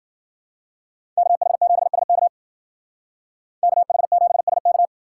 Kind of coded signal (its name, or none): Morse code